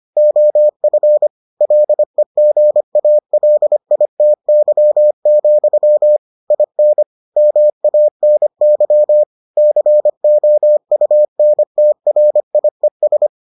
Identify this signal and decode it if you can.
Morse code, heard as a keyed tone, about 25 wpm, 600 Hz. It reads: OF LEGALITY, IN MANY COUNTRIES